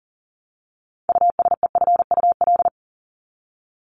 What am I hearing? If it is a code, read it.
Morse code, heard as a keyed tone, about 40 wpm, 700 Hz. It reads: USEFUL
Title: Top 501-750 Words 40wpm